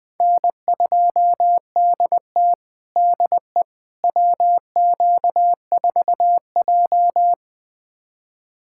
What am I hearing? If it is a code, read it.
Morse code, heard as a keyed tone, about 20 wpm, 700 Hz. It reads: N2DT DE WQ4J
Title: USA Callsign DE Callsign - 4 Characters - 20wpm